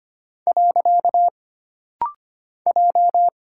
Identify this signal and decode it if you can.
Morse code, heard as a keyed tone, about 25 wpm, 700 Hz. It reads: . J
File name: Single Character (Letter:Number:Punctuation:Prosign) - Rapid-Fire - 25wpm